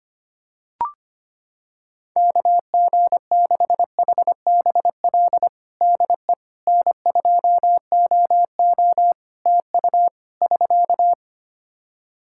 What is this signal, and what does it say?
Morse code, heard as a keyed tone, about 25 wpm, 700 Hz. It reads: KG6HBL DE N2OO TU <SK>
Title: K1USN SST QSO Practice 25wpm